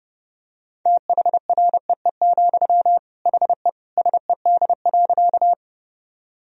Morse code, heard as a keyed tone, about 30 wpm, 700 Hz. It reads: THREE, HE SED.
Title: Sentences from Top 200 Words and Abbreviations 30wpm